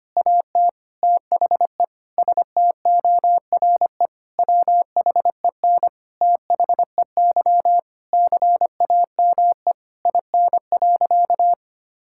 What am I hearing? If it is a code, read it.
Morse code, heard as a keyed tone, about 25 wpm, 700 Hz. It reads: AT THE STORE WHEN THEY CAME IN.
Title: Sentences from Top 1000 Words - Review Part 5 - 25wpm